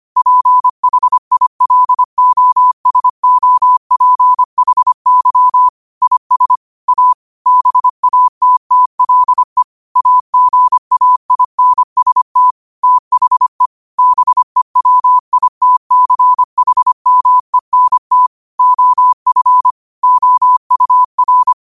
Morse code, heard as a keyed tone, about 25 wpm, 1000 Hz. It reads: PHILOSOPHY IS A BATTLE AGAINST THE BEWITCHMENT OF OUR
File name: qotdfiles / 25wpm